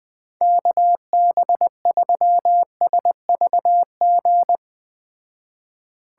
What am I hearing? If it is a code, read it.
Morse code, heard as a keyed tone, about 21 wpm, 700 Hz. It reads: KB3SVG